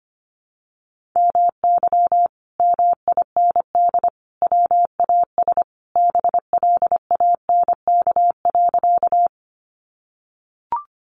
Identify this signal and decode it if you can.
Morse code, heard as a keyed tone, about 25 wpm, 700 Hz. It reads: MY MIND WAS BLANK.
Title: Sentences from Top 1500 Words 25wpm